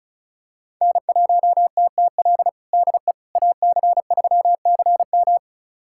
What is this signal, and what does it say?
Morse code, heard as a keyed tone, about 36 wpm, 700 Hz. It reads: N1TTL DE AC3CM
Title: USA Callsign DE Callsign - 5 Characters - 35wpm